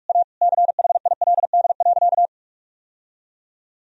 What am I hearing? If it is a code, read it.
Morse code, heard as a keyed tone, about 46 wpm, 700 Hz. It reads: A CHILD.
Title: Sentences from Top 1000 Words - Review Part 3 - 45wpm